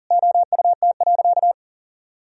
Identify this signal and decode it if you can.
Morse code, heard as a keyed tone, about 40 wpm, 700 Hz. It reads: OUT.
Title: Sentences from Top 1000 Words - Review Part 2 - 40wpm